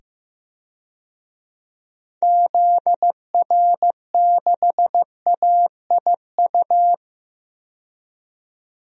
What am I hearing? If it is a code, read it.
Morse code, heard as a keyed tone, about 15 wpm, 700 Hz. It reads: ZR6AIU
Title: DXCC-100 Call Signs - 6 Characters - 15wpm